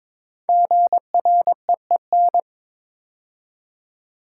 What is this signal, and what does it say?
Morse code, heard as a keyed tone, about 22 wpm, 700 Hz. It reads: GREEN
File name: 5 Letter Words - First 3 Letters Given - 22wpm